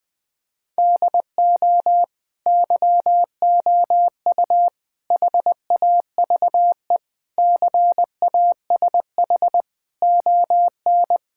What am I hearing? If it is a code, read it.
Morse code, heard as a keyed tone, about 20 wpm, 700 Hz. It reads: DO YOU HAVE CASH ON
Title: Sentences from Top 1000 Words - Review Part 5 - 20wpm